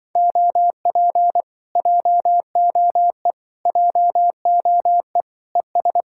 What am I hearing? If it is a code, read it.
Morse code, heard as a keyed tone, about 24 wpm, 700 Hz. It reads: OP JOE JOE ES
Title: Speed-Racing Common QSO Phrases - 20wpm